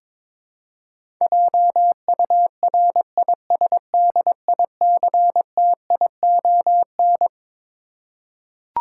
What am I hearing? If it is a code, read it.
Morse code, heard as a keyed tone, about 22 wpm, 700 Hz. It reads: JURISDICTION
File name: Twelve Letter Words - 22wpm